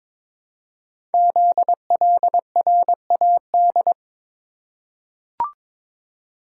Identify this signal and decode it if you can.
Morse code, heard as a keyed tone, about 22 wpm, 700 Hz. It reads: ZLRAD
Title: Code Groups - 5 Letter:Numbers - 22wpm